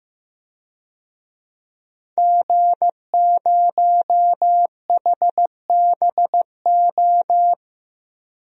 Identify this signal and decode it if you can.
Morse code, heard as a keyed tone, about 15 wpm, 700 Hz. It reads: G0HBO
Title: DXCC-100 Call Signs - 5 Characters - Repeated 3x - 15wpm